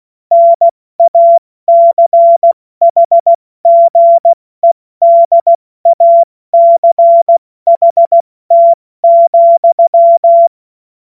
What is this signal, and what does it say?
Morse code, heard as a keyed tone, about 16 wpm, 680 Hz. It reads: NACHGEDACHT,
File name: Morsekurs / Witze / Witz 03. Mai 2022 70 F80